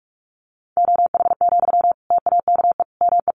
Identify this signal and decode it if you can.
Morse code, heard as a keyed tone, about 44 wpm, 700 Hz. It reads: OH, TAKE ME
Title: Sentences from Top 1400 Words 45wpm